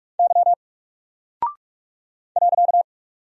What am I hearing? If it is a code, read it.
Morse code, heard as a keyed tone, about 44 wpm, 700 Hz. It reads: Y .
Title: Single Character (Letter:Number:Punctuation:Prosign) 45wpm